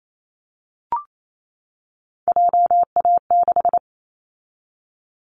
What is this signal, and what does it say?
Morse code, heard as a keyed tone, about 28 wpm, 700 Hz. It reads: JA6
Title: Callsign Prefixes for Top 25 DXCC - 28wpm